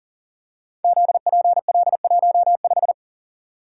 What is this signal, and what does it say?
Morse code, heard as a keyed tone, about 40 wpm, 690 Hz. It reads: ZPL15
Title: Code Groups - 5 Letter:Numbers - 40wpm